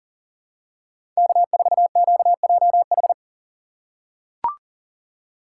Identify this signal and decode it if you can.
Morse code, heard as a keyed tone, about 40 wpm, 700 Hz. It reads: K4QJH
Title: USA Call Signs - 5 Characters - 2x Character Spacing - 40wpm